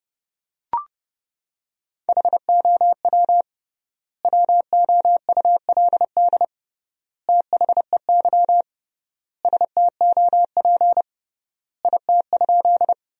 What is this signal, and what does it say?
Morse code, heard as a keyed tone, about 30 wpm, 700 Hz. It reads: HOW WOULD THEY STOP IT?
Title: 3x Word-Spacing - Sentences from Top 300 Words - 30wpm